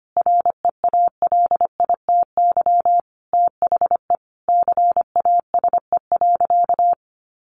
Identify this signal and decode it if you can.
Morse code, heard as a keyed tone, about 25 wpm, 700 Hz. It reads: REALITY THE CASE.
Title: Sentences from Top 1200 Words 25wpm